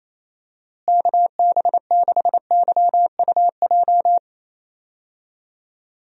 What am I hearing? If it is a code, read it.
Morse code, heard as a keyed tone, about 28 wpm, 700 Hz. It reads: KB6YUJ
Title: USA Call Signs - 6 Characters - Repeated 2x - 28wpm